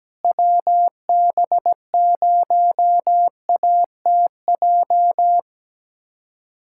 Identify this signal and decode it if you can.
Morse code, heard as a keyed tone, about 17 wpm, 700 Hz. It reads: WB0ATJ
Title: USA Callsign DE Callsign 17wpm